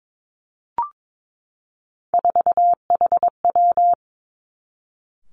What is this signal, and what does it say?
Morse code, heard as a keyed tone, about 22 wpm, 700 Hz. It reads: 4HW
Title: Three Challenging Letter-Number Combinations 22wpm